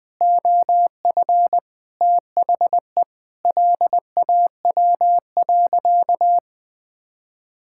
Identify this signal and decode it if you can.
Morse code, heard as a keyed tone, about 20 wpm, 700 Hz. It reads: OF THE LAW.
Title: Sentences from Top 1600 Words 20wpm